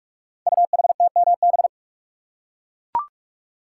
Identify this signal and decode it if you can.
Morse code, heard as a keyed tone, about 44 wpm, 700 Hz. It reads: UHTMB
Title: Code Groups - 5 Letter:Numbers - 45wpm